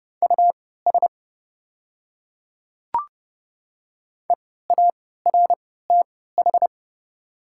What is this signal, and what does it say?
Morse code, heard as a keyed tone, about 30 wpm, spaced as for 17 wpm, 700 Hz. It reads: US EARTH
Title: 3x Character Spacing - Top 200 Words - 30wpm